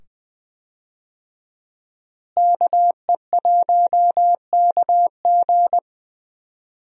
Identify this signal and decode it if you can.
Morse code, heard as a keyed tone, about 20 wpm, 700 Hz. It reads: KE1KG
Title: USA Call Signs - 5 Characters - Repeated 2x - 20wpm